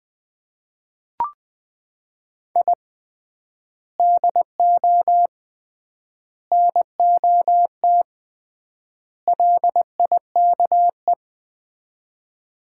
Morse code, heard as a keyed tone, about 20 wpm, 700 Hz. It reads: I DO NOT LIKE
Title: 3x Word-Spacing - Sets of 4 Words - 20wpm